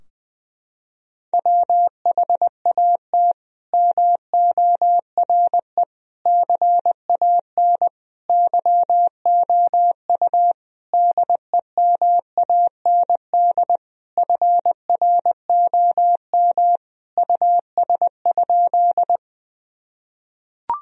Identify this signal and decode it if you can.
Morse code, heard as a keyed tone, about 21 wpm, 700 Hz. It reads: WHAT MORE CAN YOU DEMAND FROM US?
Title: Sentences from Top 1000 Words - Review Part 4 - 20wpm